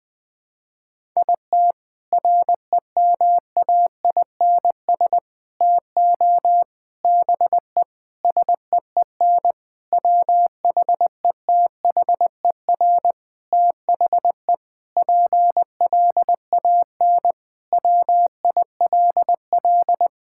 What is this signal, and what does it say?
Morse code, heard as a keyed tone, about 20 wpm, 700 Hz. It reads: IT REMAINS TO BE SEEN WHETHER THE PLAN WILL
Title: Sentences with 6-Word Phrases - 20wpm